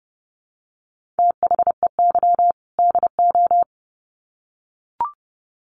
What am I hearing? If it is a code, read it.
Morse code, heard as a keyed tone, about 30 wpm, 700 Hz. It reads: THEY DO
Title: Sets of 2 Words 30wpm